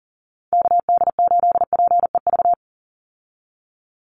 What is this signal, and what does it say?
Morse code, heard as a keyed tone, about 38 wpm, 700 Hz. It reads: KD8PEV
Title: USA Call Signs 40wpm